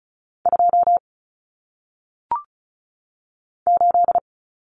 Morse code, heard as a keyed tone, about 36 wpm, 700 Hz. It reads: 2 8
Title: Common Cut Numbers - 1-Digit 35wpm